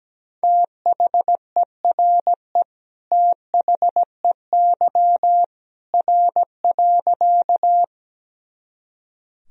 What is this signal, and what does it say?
Morse code, heard as a keyed tone, about 17 wpm, 700 Hz. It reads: THERE THEY R.
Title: Sentences from Top 100 Words and Abbreviations 17wpm